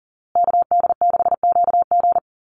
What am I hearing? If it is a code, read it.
Morse code, heard as a keyed tone, about 40 wpm, 700 Hz. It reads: KD6QG
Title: USA Call Signs Repeated 3x 40wpm